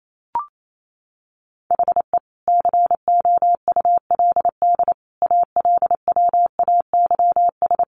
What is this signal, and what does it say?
Morse code, heard as a keyed tone, about 28 wpm, 700 Hz. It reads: HE COULD ALWAYS
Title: Sentences from Top 1400 Words 28wpm